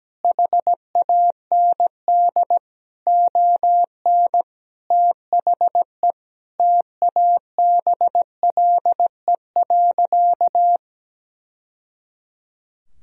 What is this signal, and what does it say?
Morse code, heard as a keyed tone, about 17 wpm, 700 Hz. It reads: HAND ON THE TABLE.
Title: Sentences from Top 1000 Words - Review Part 2 - 17wpm